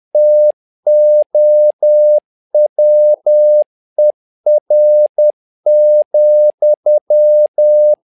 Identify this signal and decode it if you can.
Morse code, heard as a keyed tone, about 10 wpm, 600 Hz. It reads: TOWER,
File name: learning center / tower trouble / tower trouble 10wpm